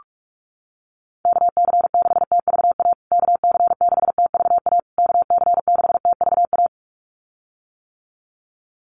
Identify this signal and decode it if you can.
Morse code, heard as a keyed tone, about 44 wpm, 690 Hz. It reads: KC6TVA KC6TVA KC6TVA
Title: USA Call Signs Repeated 3x 45wpm